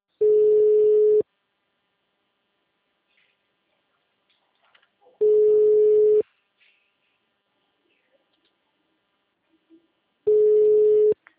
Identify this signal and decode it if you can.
Ringback tone (Europe, 425 Hz), with speech.